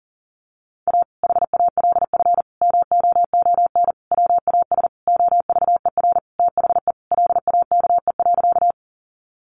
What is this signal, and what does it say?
Morse code, heard as a keyed tone, about 40 wpm, 700 Hz. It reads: A HALF MOON WAS OVER THE LAKE.
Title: Sentences from Top 1100 Words 40wpm